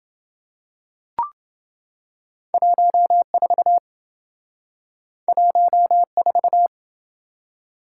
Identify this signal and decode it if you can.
Morse code, heard as a keyed tone, about 27 wpm, 700 Hz. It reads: 14 14